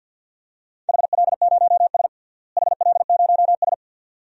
Morse code, heard as a keyed tone, about 46 wpm, 700 Hz. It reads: HL0S HL0S
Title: DXCC-100 Call Signs - 4 Characters - Repeated 2x - 50wpm